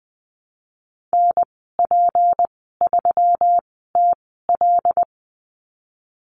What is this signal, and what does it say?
Morse code, heard as a keyed tone, about 20 wpm, spaced as for 15 wpm, 700 Hz. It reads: NP3TL
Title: 186 Lesson 30g - Letter b - Call Signs 20wpm